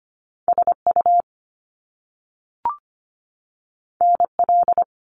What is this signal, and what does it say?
Morse code, heard as a keyed tone, about 25 wpm, 700 Hz. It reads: SU NL
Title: Top Two Letter Combinations in Words 25wpm